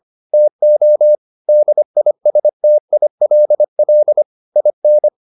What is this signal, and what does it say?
Morse code, heard as a keyed tone, about 25 wpm, 600 Hz. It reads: TO DISTILL IN